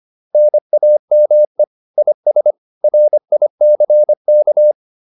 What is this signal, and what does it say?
Morse code, heard as a keyed tone, about 25 wpm, 600 Hz. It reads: NAME IS RICK